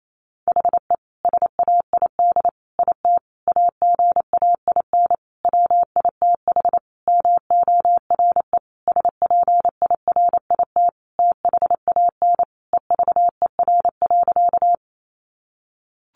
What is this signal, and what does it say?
Morse code, heard as a keyed tone, about 28 wpm, 700 Hz. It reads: HE SAID IT AGAIN WITH MORE SPIRIT THAN EVER.
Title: Sentences from Top 1400 Words 28wpm